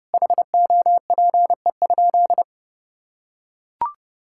Morse code, heard as a keyed tone, about 32 wpm, 700 Hz. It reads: HOPE?
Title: Sentences from Top 1000 Words - Review Part 5 - 30wpm